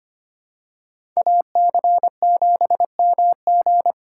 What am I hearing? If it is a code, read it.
Morse code, heard as a keyed tone, about 25 wpm, 700 Hz. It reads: AC7MG
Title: USA Call Signs - 5 Characters - 4x Character Spacing - 25wpm